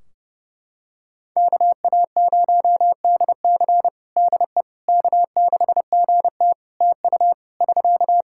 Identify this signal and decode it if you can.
Morse code, heard as a keyed tone, about 32 wpm, 700 Hz. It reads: KA0DC DE K6GT TU <SK>
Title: Practice QSO - Exchange 4 - Ending - 30wpm